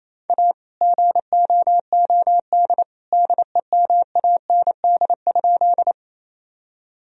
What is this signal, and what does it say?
Morse code, heard as a keyed tone, about 28 wpm, 700 Hz. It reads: A GOOD DEMAND?
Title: Sentences from Top 1000 Words - Review Part 4 - 28wpm